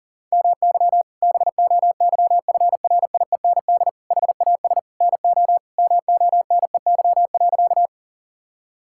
Morse code, heard as a keyed tone, about 40 wpm, 700 Hz. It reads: MY BOYFRIEND HAS NO MONEY.
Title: Sentences from Top 1000 Words - Review Part 5 - 40wpm